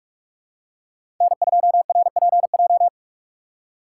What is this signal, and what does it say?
Morse code, heard as a keyed tone, about 41 wpm, 700 Hz. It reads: N2RPJ
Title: USA Call Signs - 5 Characters - 3x Character Spacing - 45wpm